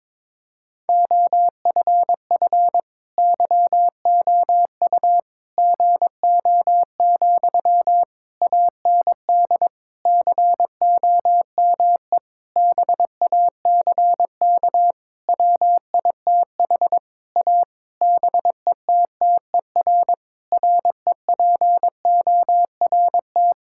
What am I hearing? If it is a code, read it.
Morse code, heard as a keyed tone, about 22 wpm, 700 Hz. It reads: OFF YOU GO, AND COME BACK WITH A BETTER REPORT
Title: Sentences from Top 1000 Words - Review Part 6 - 22wpm